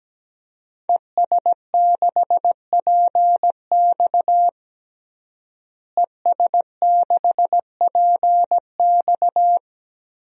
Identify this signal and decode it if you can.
Morse code, heard as a keyed tone, about 17 wpm, 700 Hz. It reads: ES6PX ES6PX
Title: DXCC-100 Call Signs - 5 Characters - Repeated 3x - 17wpm